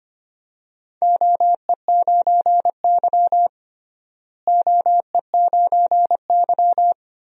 Morse code, heard as a keyed tone, about 25 wpm, 700 Hz. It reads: OE9Y OE9Y
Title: DXCC-100 Call Signs - 4 Characters - Repeated 2x - 25wpm